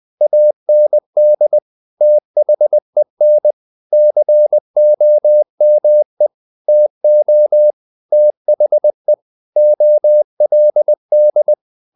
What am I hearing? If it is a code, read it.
Morse code, heard as a keyed tone, about 20 wpm, 600 Hz. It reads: AND THEN COME TO THE OLD
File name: the old man and the sea part 2 20wpm